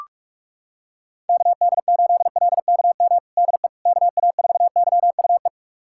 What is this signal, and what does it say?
Morse code, heard as a keyed tone, about 46 wpm, 700 Hz. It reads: KD8LKM DE KA4YUE
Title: USA Callsign DE Callsign - 6 Characters - 45wpm